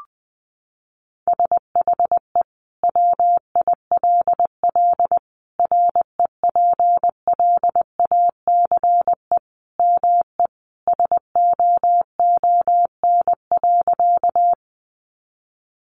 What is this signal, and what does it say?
Morse code, heard as a keyed tone, about 20 wpm, 700 Hz. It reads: SHE WILL REPLACE ME SOON.
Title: Sentences from Top 1400 Words 20wpm